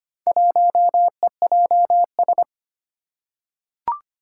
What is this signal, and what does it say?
Morse code, heard as a keyed tone, about 25 wpm, 700 Hz. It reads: 1EJS
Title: Four Challenging Letter-Number Combinations 25wpm